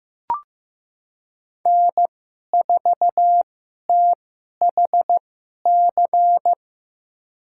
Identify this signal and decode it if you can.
Morse code, heard as a keyed tone, about 15 wpm, spaced as for 11 wpm, 700 Hz. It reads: N4THC
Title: USA Call Signs - 5 Characters - 2x Character Spacing - 15wpm